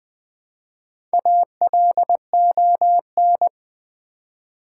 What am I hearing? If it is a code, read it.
Morse code, heard as a keyed tone, about 20 wpm, 700 Hz. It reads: ALON
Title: Build A Word - 1 Letter At A Time - 3 to 5-Letter Words - 20wpm